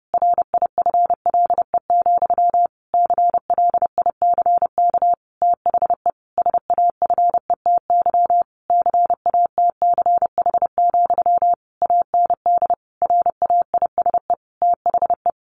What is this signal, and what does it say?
Morse code, heard as a keyed tone, about 30 wpm, 700 Hz. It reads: RIFLE, CLICK THE SAFETY CATCH, AND RAISE THE
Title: A Sound Of Thunder 30wpm